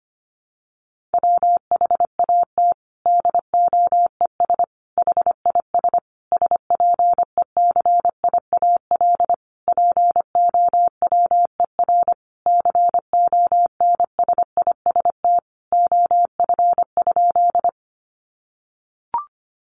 Morse code, heard as a keyed tone, about 25 wpm, 700 Hz. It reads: WHAT DOES HIS SPECIAL POWER CONSIST OF?
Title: Sentences from Top 1400 Words 25wpm